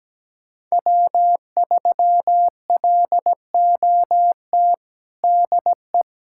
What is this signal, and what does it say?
Morse code, heard as a keyed tone, about 17 wpm, 700 Hz. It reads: W3LOT DE
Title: USA Callsign DE Callsign - 5 Characters - 17wpm